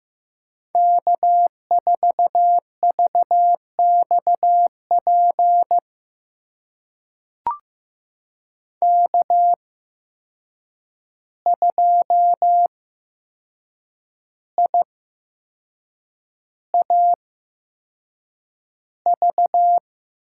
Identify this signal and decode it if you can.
Morse code, heard as a keyed tone, about 15 wpm, spaced as for 6 wpm, 700 Hz. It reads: K4VXP K2IAV
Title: USA Call Signs - 5 Characters - 8x Character Spacing - 15wpm